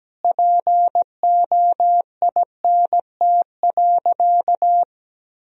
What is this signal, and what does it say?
Morse code, heard as a keyed tone, about 17 wpm, 700 Hz. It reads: POINT.
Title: Sentences from Top 400 Words and Abbreviations 17wpm